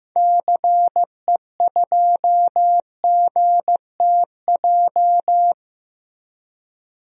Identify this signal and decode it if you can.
Morse code, heard as a keyed tone, about 15 wpm, 700 Hz. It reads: CE2GTJ
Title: DXCC-100 Call Signs - 6 Characters - 15wpm